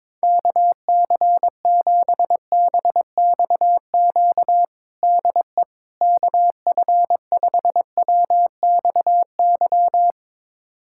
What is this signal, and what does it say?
Morse code, heard as a keyed tone, about 22 wpm, 700 Hz. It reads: KC7BXQ DE KF5WXY
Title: USA Callsign DE Callsign 22wpm